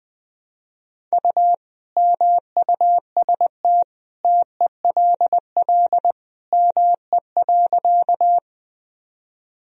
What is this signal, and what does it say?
Morse code, heard as a keyed tone, about 20 wpm, 700 Hz. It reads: U MUST TELL ME.
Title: Sentences from Top 200 Words and Abbreviations 20wpm